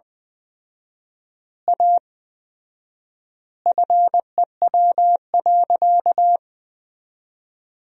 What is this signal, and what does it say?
Morse code, heard as a keyed tone, about 20 wpm, 700 Hz. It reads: A FEW.
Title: 4x Word-Spacing - Sentences from Top 300 Words - 20wpm